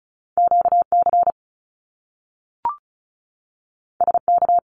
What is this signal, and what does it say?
Morse code, heard as a keyed tone, about 36 wpm, 700 Hz. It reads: QC SK
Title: Canadian Provinces and Territories 35wpm